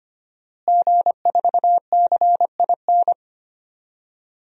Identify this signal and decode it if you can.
Morse code, heard as a keyed tone, about 25 wpm, 700 Hz. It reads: G4CIN
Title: DXCC-100 Call Signs - 5 Characters - Repeated 2x - 25wpm